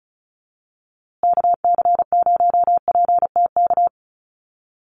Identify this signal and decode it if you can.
Morse code, heard as a keyed tone, about 34 wpm, 700 Hz. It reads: KC0PTK
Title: USA Call Signs Repeated 2x 35wpm